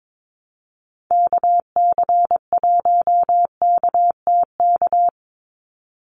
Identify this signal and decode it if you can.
Morse code, heard as a keyed tone, about 22 wpm, 700 Hz. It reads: KC1KTK
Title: USA Call Signs 22wpm